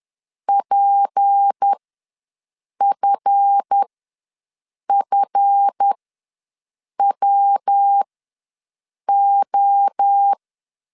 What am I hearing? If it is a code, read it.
Morse code, heard as a keyed tone, about 10 wpm, spaced as for 6 wpm, 800 Hz. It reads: PFFWO
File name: LEZIONI / 7 GF9Z / 7 GF ws 10wpm 3918